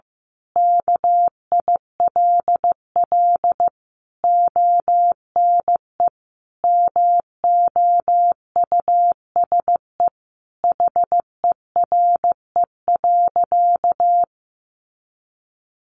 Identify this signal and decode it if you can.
Morse code, heard as a keyed tone, about 15 wpm, 700 Hz. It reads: KILL ONE MOUSE HERE.
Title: A Sound Of Thunder 15wpm